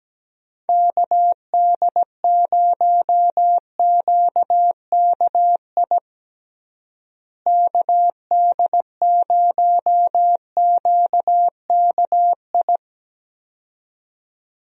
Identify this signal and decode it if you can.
Morse code, heard as a keyed tone, about 17 wpm, 700 Hz. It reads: KD0QKI KD0QKI
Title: USA Call Signs - 6 Characters - Repeated 2x - 17wpm